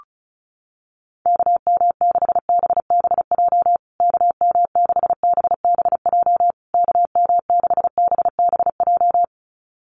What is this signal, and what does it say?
Morse code, heard as a keyed tone, about 34 wpm, 700 Hz. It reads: KM6BBJ KM6BBJ KM6BBJ
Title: USA Call Signs Repeated 3x 35wpm